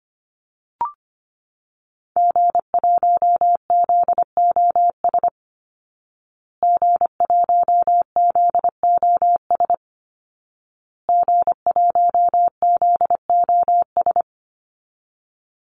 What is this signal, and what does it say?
Morse code, heard as a keyed tone, about 25 wpm, 700 Hz. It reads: G1ZOS G1ZOS G1ZOS
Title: Call Signs Repeated 2x - 25wpm